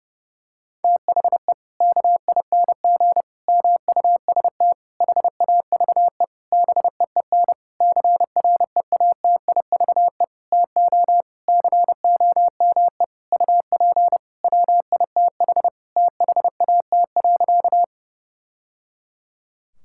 Morse code, heard as a keyed tone, about 29 wpm, 700 Hz. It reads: THE KING MUST HAVE BEEN CREATIVE TO COME UP WITH THAT.
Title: Sentences from Top 1000 Words - Review Part 5 - 30wpm